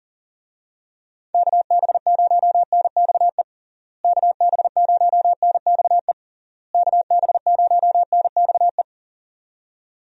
Morse code, heard as a keyed tone, about 40 wpm, 700 Hz. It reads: KB0NXE KB0NXE KB0NXE
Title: USA Call Signs - 6 Characters - Repeated 3x - 40wpm